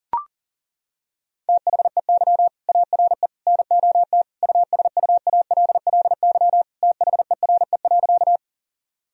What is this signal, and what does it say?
Morse code, heard as a keyed tone, about 40 wpm, 700 Hz. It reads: THEY ARE NOT USUALLY THERE.
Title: Sentences from Top 1000 Words - Review Part 3 - 40wpm